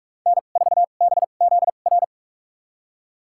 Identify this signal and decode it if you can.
Morse code, heard as a keyed tone, about 46 wpm, spaced as for 32 wpm, 700 Hz. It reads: N4BZR
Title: USA Call Signs - 2x Character Spacing - 45wpm